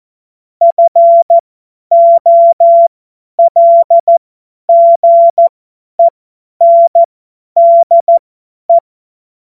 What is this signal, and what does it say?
Morse code, heard as a keyed tone, about 14 wpm, spaced as for 10 wpm, 680 Hz. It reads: FOLGENDE
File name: Morsekurs / Witze / Witz 14. März 2023 50 F70